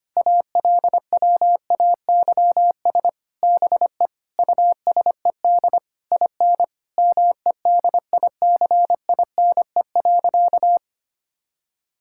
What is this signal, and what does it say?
Morse code, heard as a keyed tone, about 25 wpm, 700 Hz. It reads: ALWAYS BE USED IN MEDICINE.
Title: Sentences from Top 1000 Words - Review Part 6 - 25wpm